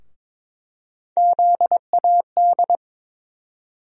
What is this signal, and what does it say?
Morse code, heard as a keyed tone, about 22 wpm, 700 Hz. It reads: ZAD
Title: USA Call Sign Postfixes - 3 Letters 22wpm